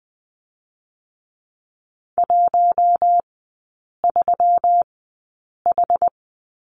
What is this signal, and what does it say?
Morse code, heard as a keyed tone, about 20 wpm, 700 Hz. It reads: 1 3 H
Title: 121 Lesson 20e - Number 3 - Three Characters 20wpm